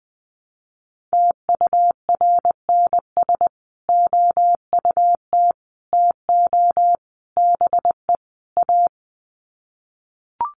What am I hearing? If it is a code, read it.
Morse code, heard as a keyed tone, about 20 wpm, 700 Hz. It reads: TURNS OUT TO BE A
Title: Sets of 5 Words Encore 20wpm